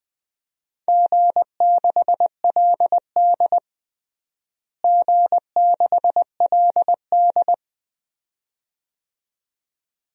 Morse code, heard as a keyed tone, about 20 wpm, 700 Hz. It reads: G6LD G6LD
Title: DXCC-100 Call Signs - 4 Characters - Repeated 2x - 20wpm